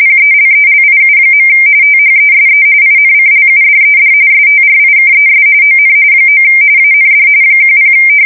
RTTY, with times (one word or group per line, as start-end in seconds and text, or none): 0.0-1.5 PELELIU.
1.8-2.6 THESE
2.8-4.1 INCLUDED
4.3-5.4 STRIKES
5.6-5.9 ON
6.3-6.4 9
6.8-7.3 AND
7.6-7.9 10